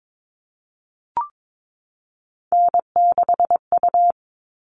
Morse code, heard as a keyed tone, about 22 wpm, 700 Hz. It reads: N6U
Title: All Cut Numbers - 3-Digit 22wpm